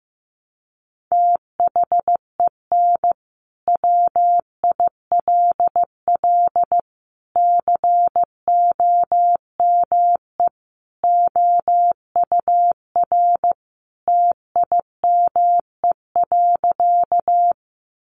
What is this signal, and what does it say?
Morse code, heard as a keyed tone, about 15 wpm, 700 Hz. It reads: THEN WILL COME OUR TIME.
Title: Sentences from Top 200 Words 15wpm